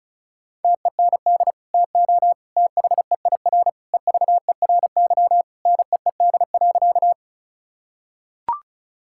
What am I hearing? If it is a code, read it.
Morse code, heard as a keyed tone, about 36 wpm, 700 Hz. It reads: TEND TO THEIR EVERY NEED.
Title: Sentences from Top 1000 Words - Review Part 5 - 35wpm